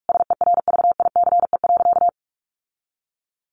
Morse code, heard as a keyed tone, about 44 wpm, 700 Hz. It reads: SERVICE.
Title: Sentences from Top 1400 Words 45wpm